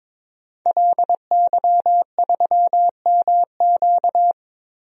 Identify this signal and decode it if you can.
Morse code, heard as a keyed tone, about 22 wpm, 700 Hz. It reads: LY3MQ
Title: DXCC-100 Call Signs - 5 Characters - 22wpm